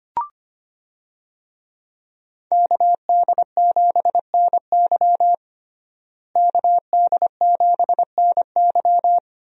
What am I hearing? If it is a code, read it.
Morse code, heard as a keyed tone, about 25 wpm, 700 Hz. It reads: KD7NY KD7NY
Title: USA Call Signs - 5 Characters - Repeated 3x - 25wpm